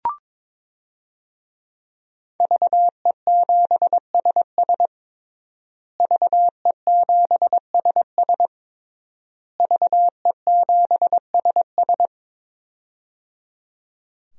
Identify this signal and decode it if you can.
Morse code, heard as a keyed tone, about 22 wpm, 700 Hz. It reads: VE7SS VE7SS VE7SS
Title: DXCC-100 Call Signs - 5 Characters - Repeated 3x - 22wpm